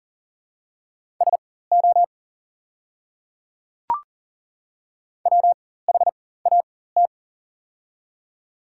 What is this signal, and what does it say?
Morse code, heard as a keyed tone, about 40 wpm, spaced as for 19 wpm, 700 Hz. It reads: SO WHAT
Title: 4x Character Spacing - Top 300 Words - 40wpm